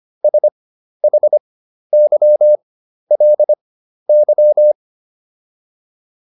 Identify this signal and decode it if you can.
Morse code, heard as a keyed tone, about 25 wpm, spaced as for 12 wpm, 600 Hz. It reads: SHYLY